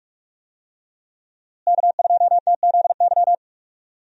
Morse code, heard as a keyed tone, about 45 wpm, 700 Hz. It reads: K2TZY
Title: USA Call Signs - 5 Characters - Repeated 3x - 45wpm